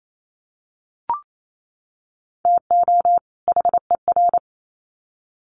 Morse code, heard as a keyed tone, about 28 wpm, 700 Hz. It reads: TO HER
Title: Sets of 2 Words 28wpm